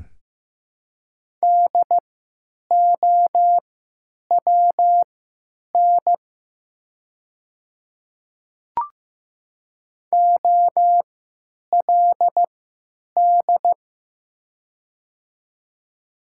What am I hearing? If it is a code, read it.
Morse code, heard as a keyed tone, about 15 wpm, spaced as for 9 wpm, 700 Hz. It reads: DOWN OLD
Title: 3x Character Spacing - Top 300 Words - 15wpm